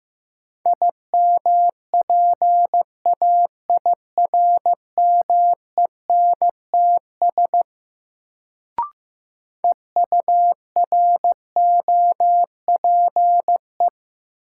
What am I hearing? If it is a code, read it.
Morse code, heard as a keyed tone, about 15 wpm, 700 Hz. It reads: IMPAIRMENTS EUROPE
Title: Sets of 5 Words Encore - New Word Drill - 15wpm